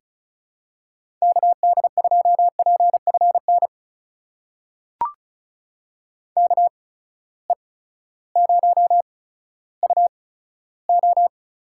Morse code, heard as a keyed tone, about 36 wpm, spaced as for 16 wpm, 700 Hz. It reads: KD2PFN KE0UO
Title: USA Call Signs - 6 Characters - 8x Character Spacing - 35wpm